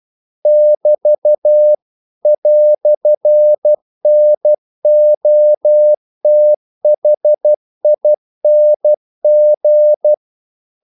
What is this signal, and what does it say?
Morse code, heard as a keyed tone, about 12 wpm, 600 Hz. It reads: <BT> "NOTHING
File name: CW / Practice / EnglishQuotations0112